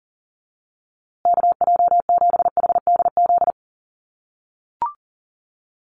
Morse code, heard as a keyed tone, about 40 wpm, 700 Hz. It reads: KJ7HDZ
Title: USA Call Signs Repeated 3x 40wpm